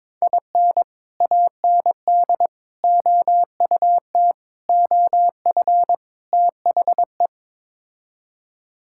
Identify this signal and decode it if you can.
Morse code, heard as a keyed tone, about 22 wpm, 700 Hz. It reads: IN AND OUT OF THE
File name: practice / 22wpm / Build A Phrase - 5-Word Phrases - 22wpm